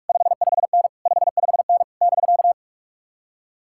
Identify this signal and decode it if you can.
Morse code, heard as a keyed tone, about 59 wpm, 700 Hz. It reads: 55N 55N <BK>